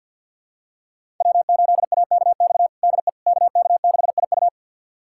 Morse code, heard as a keyed tone, about 50 wpm, 700 Hz. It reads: W8AKX DE KK6IU
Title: USA Callsign DE Callsign - 5 Characters - 50wpm